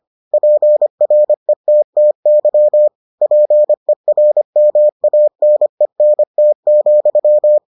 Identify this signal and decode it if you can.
Morse code, heard as a keyed tone, about 25 wpm, 600 Hz. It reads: PRETTY PERMANENT,